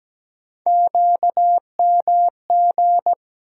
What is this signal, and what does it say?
Morse code, heard as a keyed tone, about 17 wpm, 700 Hz. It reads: QMG
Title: USA Call Sign Postfixes - 3 Letters 17wpm